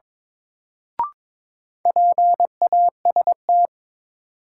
Morse code, heard as a keyed tone, about 22 wpm, 700 Hz. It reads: PAST
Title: Sentences from Top 600 Words - New Word Drill - 22wpm